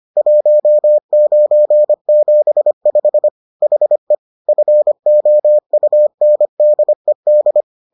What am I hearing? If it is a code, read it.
Morse code, heard as a keyed tone, about 25 wpm, 600 Hz. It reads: 1975 HE FOUNDED